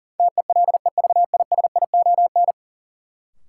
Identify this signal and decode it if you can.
Morse code, heard as a keyed tone, about 40 wpm, 700 Hz. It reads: TELEVISION
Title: Sentences from Top 1000 Words - Review Part 4 - New Word Drill - 40wpm